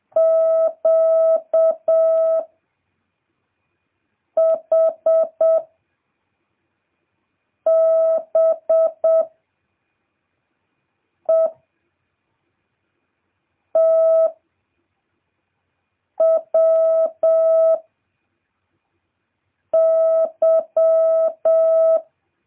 Morse code, heard as a keyed tone, about 7 wpm, 650 Hz. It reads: Q H B E T W Y